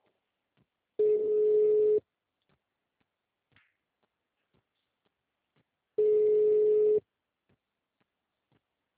Ringback tone (Europe, 425 Hz).